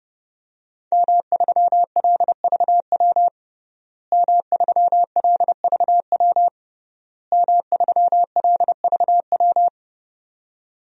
Morse code, heard as a keyed tone, about 29 wpm, 700 Hz. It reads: M3LVW M3LVW M3LVW
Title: DXCC-100 Call Signs - 5 Characters - Repeated 3x - 30wpm